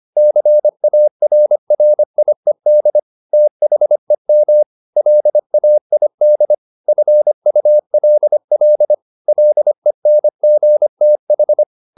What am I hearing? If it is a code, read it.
Morse code, heard as a keyed tone, about 25 wpm, 600 Hz. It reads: CARRIED THEM LAID FULL LENGTH